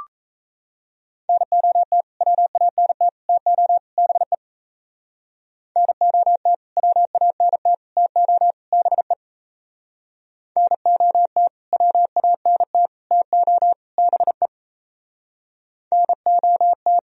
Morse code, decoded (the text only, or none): NOT WANT TO BE NOT WANT TO BE NOT WANT TO BE NOT